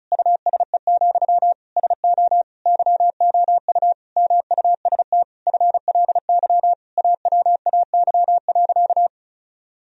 Morse code, heard as a keyed tone, about 34 wpm, 700 Hz. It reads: USE, SO YOU MUST FLY AWAY.
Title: Sentences from Top 1000 Words - Review Part 2 - 35wpm